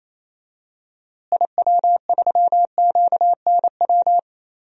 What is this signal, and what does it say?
Morse code, heard as a keyed tone, about 28 wpm, 700 Hz. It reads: IW3QNW
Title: DXCC-100 Call Signs - 6 Characters - Repeated 3x - 28wpm